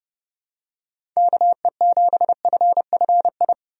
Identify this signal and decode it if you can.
Morse code, heard as a keyed tone, about 32 wpm, 700 Hz. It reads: KE7FFI
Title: USA Call Signs - 6 Characters - 30wpm